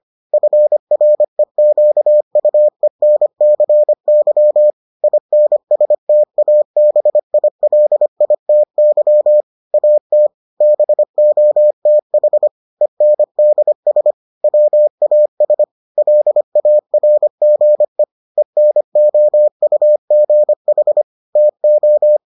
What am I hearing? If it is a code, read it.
Morse code, heard as a keyed tone, about 25 wpm, 600 Hz. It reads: FREQUENCY INSTABILITY AT BOTH ENDS WAS LARGE ENOUGH TO